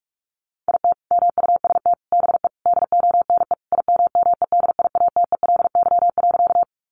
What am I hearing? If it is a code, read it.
Morse code, heard as a keyed tone, about 44 wpm, 700 Hz. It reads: IT MUST BE DONE IMMEDIATELY.
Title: Sentences from Top 1100 Words 45wpm